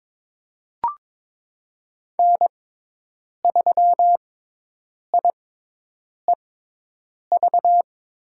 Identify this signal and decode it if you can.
Morse code, heard as a keyed tone, about 22 wpm, spaced as for 8 wpm, 700 Hz. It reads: N3IEV
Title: USA Call Signs - 5 Characters - 6x Character Spacing - 22wpm